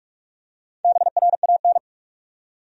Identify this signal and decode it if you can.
Morse code, heard as a keyed tone, about 47 wpm, 700 Hz. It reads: BRAN
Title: USA Common Names - Max 5 Letters - 45wpm